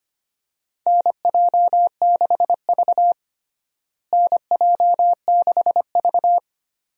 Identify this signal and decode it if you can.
Morse code, heard as a keyed tone, about 25 wpm, 700 Hz. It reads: NJ6V NJ6V
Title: USA Call Signs - 4 Characters - Repeated 2x - 25wpm